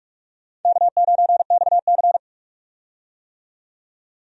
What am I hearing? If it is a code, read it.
Morse code, heard as a keyed tone, about 41 wpm, 700 Hz. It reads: K9XC
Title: USA Call Signs - 4 Characters - 45wpm